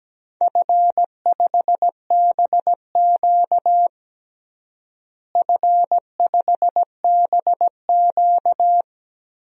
Morse code, heard as a keyed tone, about 17 wpm, 700 Hz. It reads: F5BQ F5BQ
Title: DXCC-100 Call Signs - 4 Characters - Repeated 2x - 17wpm